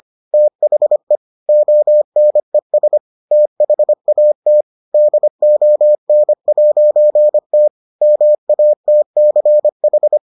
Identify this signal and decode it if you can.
Morse code, heard as a keyed tone, about 25 wpm, 600 Hz. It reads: THE ONES THAT DON'T MATCH